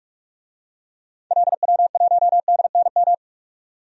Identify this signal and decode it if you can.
Morse code, heard as a keyed tone, about 46 wpm, 700 Hz. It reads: LW1DNM
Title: DXCC-100 Call Signs - 6 Characters - 45wpm